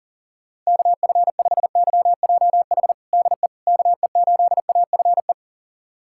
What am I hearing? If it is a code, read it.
Morse code, heard as a keyed tone, about 40 wpm, 700 Hz. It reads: KF5YJH DE KE8AFE
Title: USA Callsign DE Callsign - 6 Characters - 40wpm